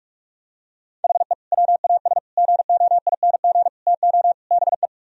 Morse code, heard as a keyed tone, about 46 wpm, 700 Hz. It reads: HE WAS GOING TO BE